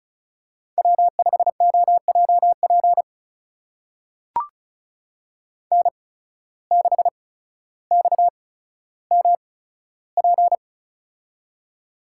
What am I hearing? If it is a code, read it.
Morse code, heard as a keyed tone, about 36 wpm, spaced as for 15 wpm, 700 Hz. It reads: W5OJP N6XMP
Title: USA Call Signs - 5 Characters - 8x Character Spacing - 35wpm